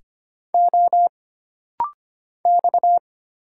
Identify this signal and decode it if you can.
Morse code, heard as a keyed tone, about 25 wpm, 700 Hz. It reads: O X
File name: Single Character (Letter:Number:Punctuation:Prosign) - Rapid-Fire - 25wpm